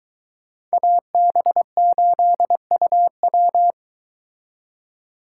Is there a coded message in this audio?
Yes: Morse code from 0.7 to 3.7 s, heard as a keyed tone, about 22 wpm, 700 Hz. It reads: AB8UW